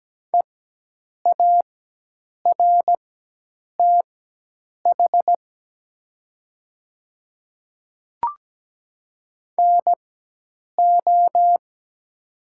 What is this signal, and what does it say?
Morse code, heard as a keyed tone, about 17 wpm, spaced as for 8 wpm, 700 Hz. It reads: EARTH NO
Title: 4x Character Spacing - Top 300 Words - 17wpm